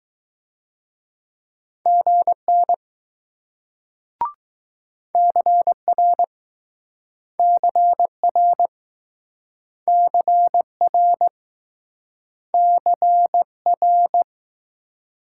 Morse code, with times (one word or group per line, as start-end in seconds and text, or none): 1.9-2.7 GN
5.1-6.2 CR
7.4-8.7 CR
9.9-11.3 CR
12.5-14.2 CR